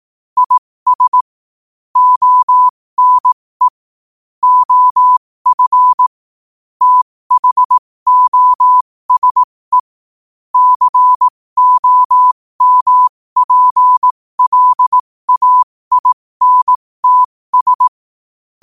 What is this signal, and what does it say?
Morse code, heard as a keyed tone, about 18 wpm, spaced as for 15 wpm, 1000 Hz. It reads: IS ONE OF THOSE COMPLAINTS
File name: qotdfiles / 15wpm